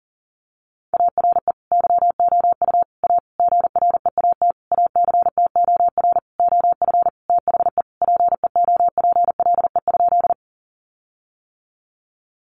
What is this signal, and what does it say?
Morse code, heard as a keyed tone, about 40 wpm, 700 Hz. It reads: ARE YOU A GREAT ACTOR OF THE PEOPLE?
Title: Sentences from Top 1500 Words 40wpm